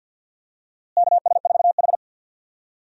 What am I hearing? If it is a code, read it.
Morse code, heard as a keyed tone, about 51 wpm, 700 Hz. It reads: KS4H
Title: USA Call Signs - 4 Characters - Repeated 3x - 50wpm